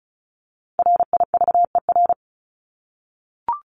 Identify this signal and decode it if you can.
Morse code, heard as a keyed tone, about 34 wpm, 700 Hz. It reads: RIVER
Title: Top 500 Words 35wpm